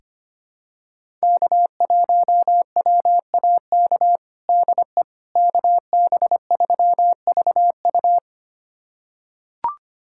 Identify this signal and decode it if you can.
Morse code, heard as a keyed tone, about 25 wpm, 700 Hz. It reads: K1WAK DE KB3VU
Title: USA Callsign DE Callsign - 5 Characters - 25wpm